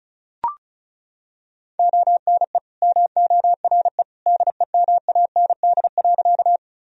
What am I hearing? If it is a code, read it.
Morse code, heard as a keyed tone, about 34 wpm, 700 Hz. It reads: ONE MORE DEMAND.
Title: Sentences from Top 1000 Words - Review Part 4 - 35wpm